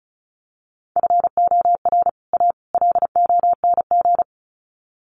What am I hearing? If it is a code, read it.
Morse code, heard as a keyed tone, about 34 wpm, 700 Hz. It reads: FOR A LONG
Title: Sets of 3 Words 35wpm